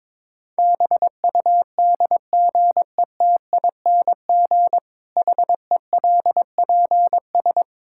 Morse code, heard as a keyed tone, about 22 wpm, 700 Hz. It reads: BUDGETING HELPS
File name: Difficult Sentences with Hints - 22wpm